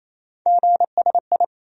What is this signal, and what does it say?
Morse code, heard as a keyed tone, about 28 wpm, 700 Hz. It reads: GSI
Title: USA Call Sign Postfixes - 3 Letters 28wpm